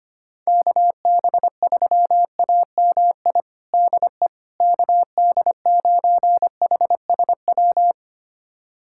Morse code, heard as a keyed tone, about 25 wpm, 700 Hz. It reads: KB3AMI DE KD9HSW
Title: USA Callsign DE Callsign - 6 Characters - 25wpm